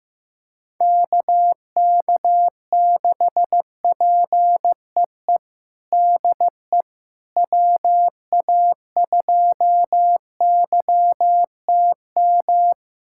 Morse code, heard as a keyed tone, about 15 wpm, 700 Hz. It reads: KK6PEE DE WA2YTM
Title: USA Callsign DE Callsign - 6 Characters - 15wpm